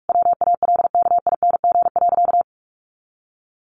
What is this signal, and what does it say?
Morse code, heard as a keyed tone, about 44 wpm, 700 Hz. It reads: WALKING.
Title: Additional Easy Phrases 45wpm